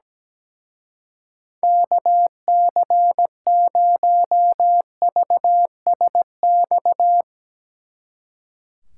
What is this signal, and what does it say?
Morse code, heard as a keyed tone, about 17 wpm, 700 Hz. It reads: KC0VSX
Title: Speed Racing - USA 2x3 Call Signs 17wpm